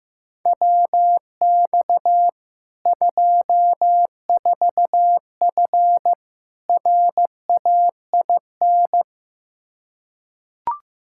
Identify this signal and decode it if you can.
Morse code, heard as a keyed tone, about 16 wpm, 700 Hz. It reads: WX 24F RAIN
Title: Practice QSO - Exchange 2 - Info - 15wpm